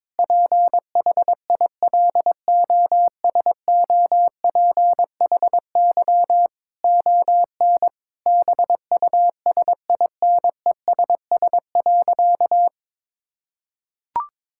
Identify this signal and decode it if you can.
Morse code, heard as a keyed tone, about 22 wpm, 700 Hz. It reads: PHILOSOPHY ON BUSINESS.
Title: Sentences from Top 1000 Words - Review Part 6 - 22wpm